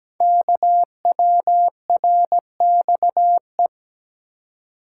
Morse code, heard as a keyed tone, about 17 wpm, 700 Hz. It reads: KWRXE
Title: Code Groups - 5 Letters - 17wpm